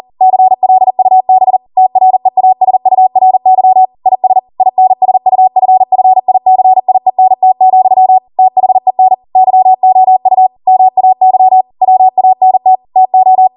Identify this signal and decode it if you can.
Morse code, heard as a keyed tone, about 40 wpm, 750 Hz. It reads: CLUB TREASURY IS INSUFFICIENT, THEN YOU MAY WANT TO